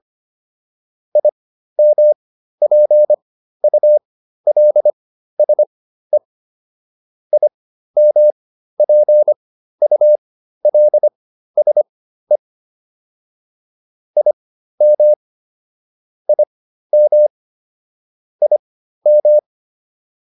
Morse code, heard as a keyed tone, about 25 wpm, spaced as for 13 wpm, 600 Hz. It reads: IMPULSE IMPULSE IM IM IM
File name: IM7 13